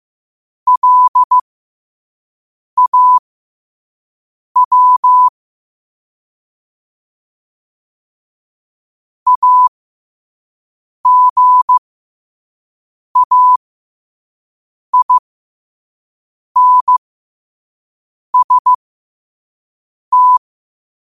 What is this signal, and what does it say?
Morse code, heard as a keyed tone, about 15 wpm, spaced as for 5 wpm, 1000 Hz. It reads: LAW AGAINST
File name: qotdfiles / 5wpm